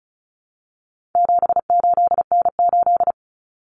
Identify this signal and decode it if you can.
Morse code, heard as a keyed tone, about 34 wpm, 700 Hz. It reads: 78N8
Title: Common Cut Numbers - 4-Digit 35wpm